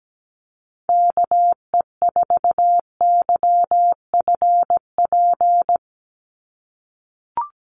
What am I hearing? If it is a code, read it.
Morse code, heard as a keyed tone, about 17 wpm, 700 Hz. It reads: KE4YFP
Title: USA Call Signs 17wpm